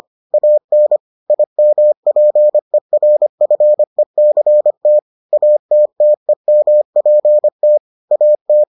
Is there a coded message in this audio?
Yes: Morse code from 0.3 to 8.6 s, heard as a keyed tone, about 25 wpm, 600 Hz. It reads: AN IMPERFECT ATTEMPT AT